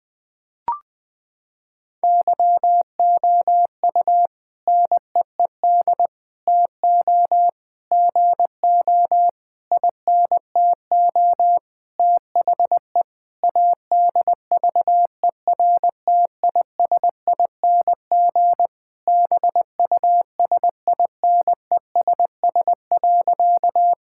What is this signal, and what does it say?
Morse code, heard as a keyed tone, about 20 wpm, 700 Hz. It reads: YOU NEED TO GO INTO THE ADVERTISING BUSINESS.
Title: Sentences from Top 1000 Words - Review Part 6 - 20wpm